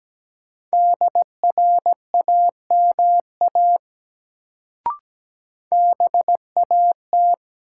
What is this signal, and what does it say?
Morse code, heard as a keyed tone, about 17 wpm, 700 Hz. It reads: DRAMA BAT
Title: Sentences from Top 1500 Words - New Word Drill Alt - 17wpm